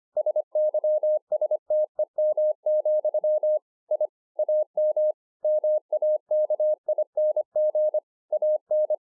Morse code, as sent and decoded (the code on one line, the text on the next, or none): ... -.-- ... - . -- --..-- / .. / .- -- / -- .- -.- .. -. --. / .- -.
SYSTEM, I AM MAKING AN